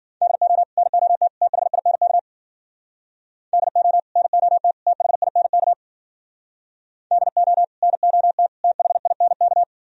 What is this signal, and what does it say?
Morse code, heard as a keyed tone, about 52 wpm, 700 Hz. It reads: DO NOT THINK DO NOT THINK DO NOT THINK